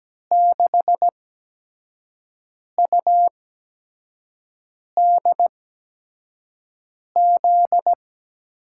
Morse code, heard as a keyed tone, about 17 wpm, spaced as for 5 wpm, 700 Hz. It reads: 6UDZ
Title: USA Call Signs - 8x Character Spacing - 17wpm